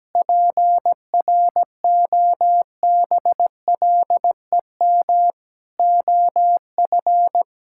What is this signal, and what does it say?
Morse code, heard as a keyed tone, about 17 wpm, 700 Hz. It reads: PROBLEM OF
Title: Sentences from Top 1000 Words - Review Part 5 - 17wpm